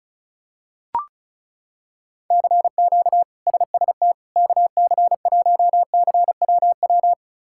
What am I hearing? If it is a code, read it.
Morse code, heard as a keyed tone, about 36 wpm, 700 Hz. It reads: CQ SST KC1CWW
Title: K1USN SST QSO Practice - 35wpm